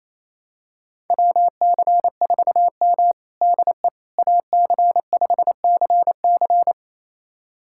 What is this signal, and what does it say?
Morse code, heard as a keyed tone, about 28 wpm, 700 Hz. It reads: WC4M DE AC5CC
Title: USA Callsign DE Callsign 28wpm